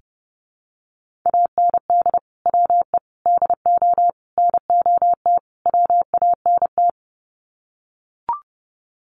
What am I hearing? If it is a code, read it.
Morse code, heard as a keyed tone, about 28 wpm, 700 Hz. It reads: AND WE DO NOT WANT
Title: Sets of 5 Words Encore2 30wpm